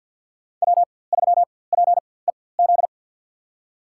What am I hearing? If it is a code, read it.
Morse code, heard as a keyed tone, about 51 wpm, spaced as for 23 wpm, 700 Hz. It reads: W3PEZ
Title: USA Call Signs - 5 Characters - 4x Character Spacing - 50wpm